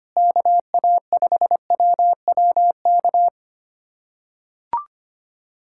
Morse code, heard as a keyed tone, about 25 wpm, 700 Hz. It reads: KA5WWK
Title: USA Callsign DE Callsign 25wpm